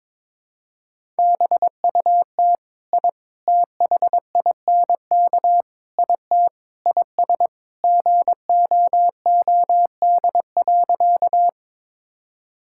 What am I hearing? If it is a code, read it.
Morse code, heard as a keyed tone, about 22 wpm, 700 Hz. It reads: BUT I THINK IT IS GOOD.
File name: practice / 22wpm / Sentences with 4-Word Phrases - 22wpm